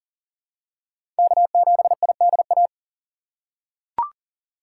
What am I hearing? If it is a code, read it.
Morse code, heard as a keyed tone, about 40 wpm, 700 Hz. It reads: K7IDA
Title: USA Call Signs - 5 Characters - 6x Character Spacing - 40wpm